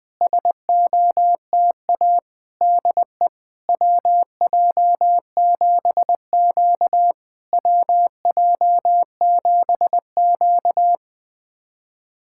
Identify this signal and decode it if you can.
Morse code, heard as a keyed tone, about 21 wpm, 690 Hz. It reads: SOTA DE WJ7Q WJ7Q
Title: SOTA QSO Practice 20wpm